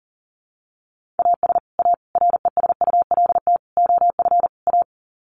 Morse code, heard as a keyed tone, about 40 wpm, 700 Hz. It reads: AS A RESULT OF A
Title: Sets of 5 Words Encore 40wpm